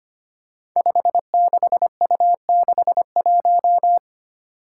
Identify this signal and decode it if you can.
Morse code, heard as a keyed tone, about 25 wpm, 700 Hz. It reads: 56U61
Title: Five Challenging Letter-Number Combinations 25wpm